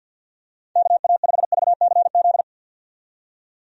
Morse code, heard as a keyed tone, about 50 wpm, 700 Hz. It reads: KA5VCZ
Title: USA Call Signs - 6 Characters - 50wpm